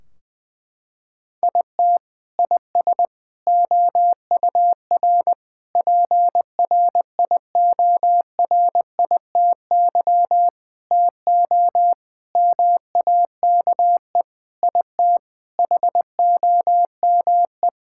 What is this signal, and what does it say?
Morse code, heard as a keyed tone, about 21 wpm, 700 Hz. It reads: IT IS OUR PRIORITY TO MAKE IT HOME
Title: Sentences from Top 1600 Words 20wpm